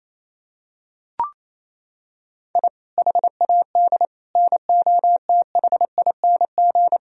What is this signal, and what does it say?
Morse code, heard as a keyed tone, about 28 wpm, 700 Hz. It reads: I HAD NOTHING
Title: Sentences from Top 1000 Words - Review Part 1 - 28wpm